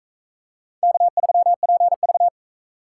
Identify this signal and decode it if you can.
Morse code, heard as a keyed tone, about 40 wpm, 700 Hz. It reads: K3PV